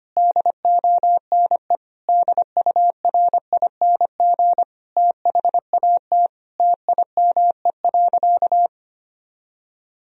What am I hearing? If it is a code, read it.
Morse code, heard as a keyed tone, about 25 wpm, 700 Hz. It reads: DONE DURING THAT TIME.
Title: Sentences from Top 1000 Words - Review Part 2 - 25wpm